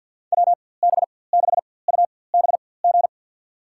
Morse code, heard as a keyed tone, about 46 wpm, spaced as for 23 wpm, 700 Hz. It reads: WB6UBG
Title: USA Call Signs - 6 Characters - 4x Character Spacing - 50wpm